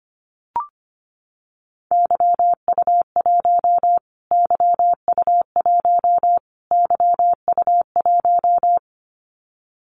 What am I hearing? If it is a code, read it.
Morse code, heard as a keyed tone, about 25 wpm, 700 Hz. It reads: YU1 YU1 YU1
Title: Callsign Prefixes for Top 25 DXCC - Repeated 3x - 25wpm